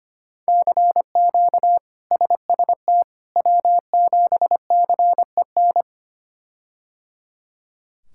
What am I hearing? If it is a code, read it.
Morse code, heard as a keyed tone, about 25 wpm, 700 Hz. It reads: CQ SST W7CEN
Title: K1USN SST QSO Practice - 25wpm